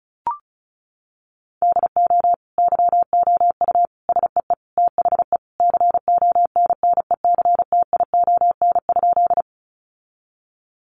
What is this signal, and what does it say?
Morse code, heard as a keyed tone, about 34 wpm, 700 Hz. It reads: DO YOU SEE THE CONNECTION?
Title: Sentences from Top 1300 Words 35wpm